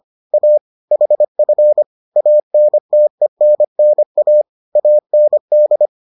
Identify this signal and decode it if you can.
Morse code, heard as a keyed tone, about 25 wpm, 600 Hz. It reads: A HF ANTENNA AND